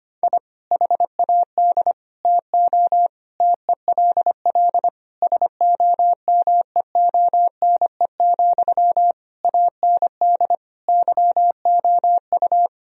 Morse code, heard as a keyed tone, about 25 wpm, 700 Hz. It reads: I HAD TO TELL SOMEONE, AND YOU
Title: Sentences from Top 1000 Words - Review Part 4 - 25wpm